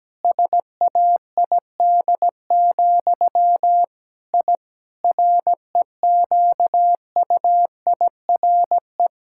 Morse code, heard as a keyed tone, about 17 wpm, 700 Hz. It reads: SAID, I REQUIRE
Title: Sentences from Top 1000 Words - Review Part 6 - 17wpm